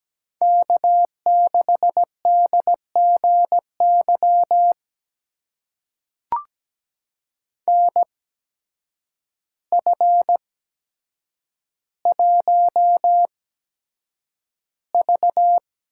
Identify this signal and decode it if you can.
Morse code, heard as a keyed tone, about 17 wpm, spaced as for 8 wpm, 700 Hz. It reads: K6DGY NF1V
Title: USA Call Signs - 8x Character Spacing - 17wpm